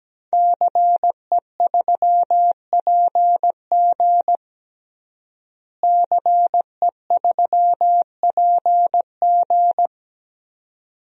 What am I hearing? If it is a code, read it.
Morse code, heard as a keyed tone, about 17 wpm, 700 Hz. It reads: CE3PG CE3PG
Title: DXCC-100 Call Signs - 5 Characters - Repeated 2x - 17wpm